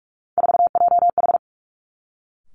Morse code, heard as a keyed tone, about 46 wpm, 700 Hz. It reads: 4JH
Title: Three Challenging Letter-Number Combinations 45wpm